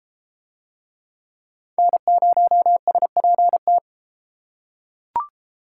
Morse code, heard as a keyed tone, about 30 wpm, 700 Hz. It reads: N0SPT